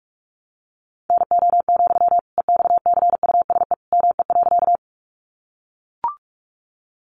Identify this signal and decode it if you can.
Morse code, heard as a keyed tone, about 44 wpm, 700 Hz. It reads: NO, EXCUSE ME.
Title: Sentences from Top 700 Words 45wpm